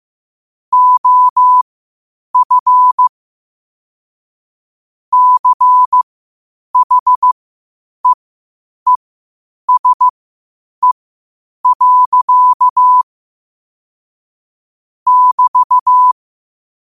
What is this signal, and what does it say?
Morse code, heard as a keyed tone, about 15 wpm, spaced as for 8 wpm, 1000 Hz. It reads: OF CHEESE. <BT>
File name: qotdfiles / 8wpm